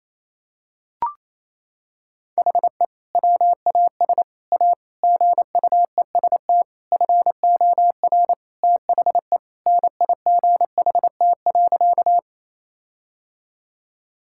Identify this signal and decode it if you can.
Morse code, heard as a keyed tone, about 28 wpm, 700 Hz. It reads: HE WAS A GUEST FOR THE NIGHT.
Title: Sentences from Top 1600 Words 28wpm